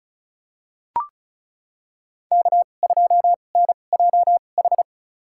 Morse code, heard as a keyed tone, about 34 wpm, spaced as for 25 wpm, 700 Hz. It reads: K2NJH
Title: USA Call Signs - 2x Character Spacing - 35wpm